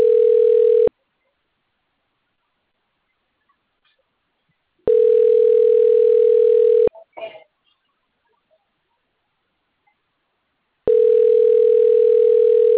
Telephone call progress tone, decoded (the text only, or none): ringback tone